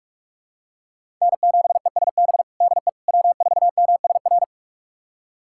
Morse code, heard as a keyed tone, about 42 wpm, 700 Hz. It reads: N7ESB DE W4MSR
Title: USA Callsign DE Callsign - 5 Characters - 45wpm